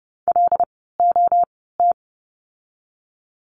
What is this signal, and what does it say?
Morse code, heard as a keyed tone, about 30 wpm, spaced as for 17 wpm, 700 Hz. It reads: LOT
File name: course / 30:10wpm / 108 Lesson 18f - Letter m - Words 30wpm